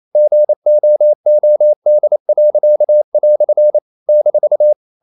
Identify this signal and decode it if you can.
Morse code, heard as a keyed tone, about 28 wpm, 600 Hz. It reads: GOOD." -
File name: CW / Practice / EnglishQuotations0228